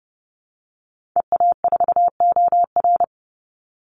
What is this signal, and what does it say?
Morse code, heard as a keyed tone, about 30 wpm, 700 Hz. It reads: EA4OR
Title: Call Signs Repeated 2x - 30wpm